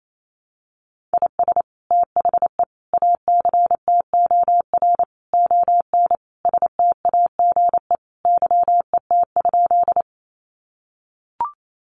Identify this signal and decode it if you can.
Morse code, heard as a keyed tone, about 28 wpm, 700 Hz. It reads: IS THE ACTOR ON STAGE YET?
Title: Sentences from Top 1500 Words 28wpm